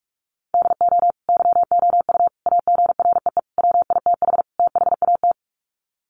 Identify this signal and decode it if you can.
Morse code, heard as a keyed tone, about 44 wpm, 700 Hz. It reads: DO YOU AGREE WITH THAT
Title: Sets of 5 Words Encore 45wpm